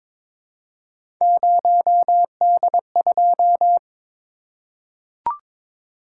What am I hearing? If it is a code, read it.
Morse code, heard as a keyed tone, about 22 wpm, 700 Hz. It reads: 0D2
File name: practice / 22wpm / Three Letter-Number Combinations - 22wpm